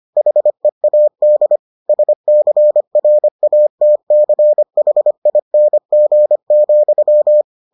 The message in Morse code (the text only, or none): HEAD SCRATCHING,